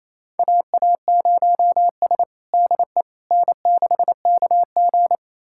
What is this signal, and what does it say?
Morse code, heard as a keyed tone, about 28 wpm, 700 Hz. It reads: AA0S DE N6KG
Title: USA Callsign DE Callsign - 4 Characters - 28wpm